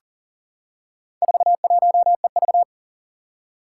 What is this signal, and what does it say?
Morse code, heard as a keyed tone, about 40 wpm, 700 Hz. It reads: 41EV
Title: Four Challenging Letter-Number Combinations 40wpm